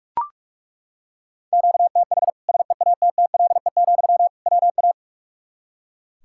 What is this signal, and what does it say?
Morse code, heard as a keyed tone, about 41 wpm, 700 Hz. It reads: QTH SEATTLE, WA
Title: Practice QSO - Exchange 1 - Beginning - 45wpm